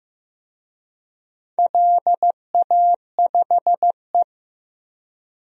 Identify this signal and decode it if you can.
Morse code, heard as a keyed tone, about 15 wpm, 700 Hz. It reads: LA5E
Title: DXCC-100 Call Signs - 4 Characters - Repeated 3x - 15wpm